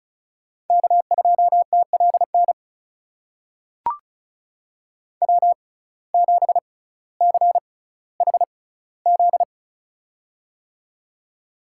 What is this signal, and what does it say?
Morse code, heard as a keyed tone, about 36 wpm, spaced as for 18 wpm, 700 Hz. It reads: K2TLN W7CHZ
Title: USA Call Signs - 5 Characters - 6x Character Spacing - 35wpm